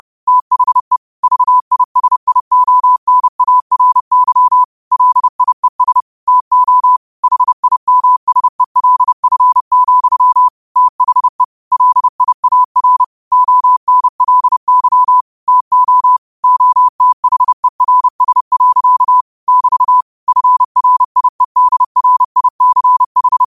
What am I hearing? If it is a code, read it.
Morse code, heard as a keyed tone, about 30 wpm, 1000 Hz. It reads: THE VISIONARY LIES TO HIMSELF, THE LIAR ONLY TO OTHERS. <BT> FRIEDRICH
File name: qotdfiles / 30wpm